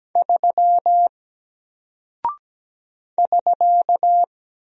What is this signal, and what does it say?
Morse code, heard as a keyed tone, about 17 wpm, 700 Hz. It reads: 3 <SK>
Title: Single Character (Letter:Number:Punctuation:Prosign) 17wpm